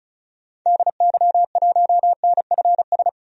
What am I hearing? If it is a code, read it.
Morse code, heard as a keyed tone, about 36 wpm, 700 Hz. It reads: DY1NFS
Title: DXCC-100 Call Signs - 6 Characters - 35wpm